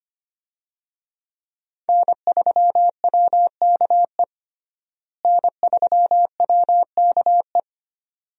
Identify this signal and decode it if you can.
Morse code, heard as a keyed tone, about 25 wpm, 700 Hz. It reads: N3WKE N3WKE
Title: USA Call Signs - 5 Characters - Repeated 2x - 25wpm